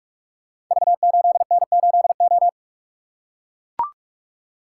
Morse code, heard as a keyed tone, about 44 wpm, 700 Hz. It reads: V8N8O
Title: Code Groups - 5 Letter:Numbers - 45wpm